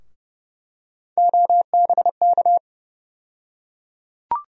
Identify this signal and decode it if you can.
Morse code, heard as a keyed tone, about 31 wpm, 700 Hz. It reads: OBK
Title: USA Call Sign Postfixes - 3 Letters 30wpm